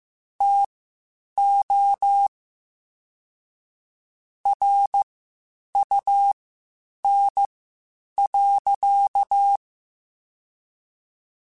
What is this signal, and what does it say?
Morse code, heard as a keyed tone, about 15 wpm, spaced as for 8 wpm, 800 Hz. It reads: TO RUN.